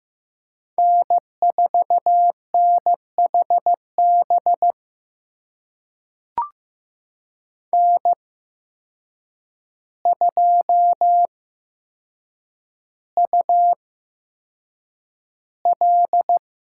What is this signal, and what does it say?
Morse code, heard as a keyed tone, about 15 wpm, spaced as for 7 wpm, 700 Hz. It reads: N4NHB N2UL
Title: USA Call Signs - 5 Characters - 8x Character Spacing - 15wpm